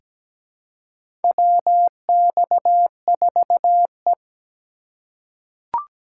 Morse code, heard as a keyed tone, about 17 wpm, 700 Hz. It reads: WX4E
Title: USA Call Signs - 4 Characters - 4x Character Spacing - 17wpm